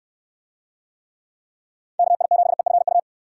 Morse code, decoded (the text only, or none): KI7EFU